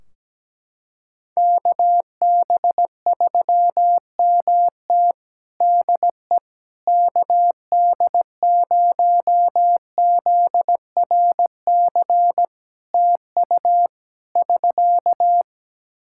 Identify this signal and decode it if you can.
Morse code, heard as a keyed tone, about 17 wpm, 700 Hz. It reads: KB3MT DE KD0ZRC TU <SK>
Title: Practice QSO - Exchange 4 - Ending - 17wpm